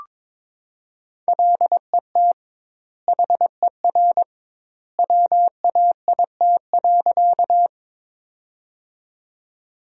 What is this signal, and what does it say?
Morse code, heard as a keyed tone, about 22 wpm, 700 Hz. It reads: LET HER WAIT.
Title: 2x Word-Spacing - Sentences from Top 500 Words - 22wpm